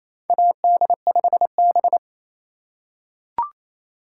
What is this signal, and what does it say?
Morse code, heard as a keyed tone, about 28 wpm, 700 Hz. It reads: AD5B
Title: USA Call Signs - 4 Characters - 8x Character Spacing - 28wpm